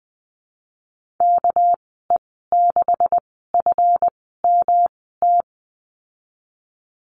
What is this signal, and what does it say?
Morse code, heard as a keyed tone, about 20 wpm, spaced as for 15 wpm, 700 Hz. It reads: KE6FMT
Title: 200 Lesson 32g - Letter k - Call Signs 20wpm